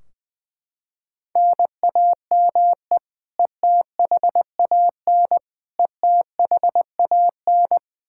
Morse code, heard as a keyed tone, about 20 wpm, 700 Hz. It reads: NAME ETHAN ETHAN
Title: Practice QSO - Exchange 1 - Beginning - 20wpm